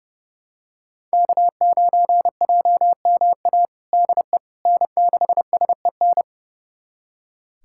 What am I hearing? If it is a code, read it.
Morse code, heard as a keyed tone, about 32 wpm, 700 Hz. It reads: K9JMA DE N6SEN
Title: USA Callsign DE Callsign - 5 Characters - 30wpm